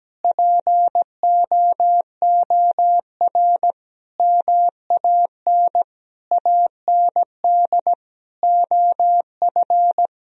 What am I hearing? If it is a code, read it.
Morse code, heard as a keyed tone, about 17 wpm, 700 Hz. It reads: POOR MAN AND OF
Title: Sentences from Top 1000 Words - Review Part 3 - 17wpm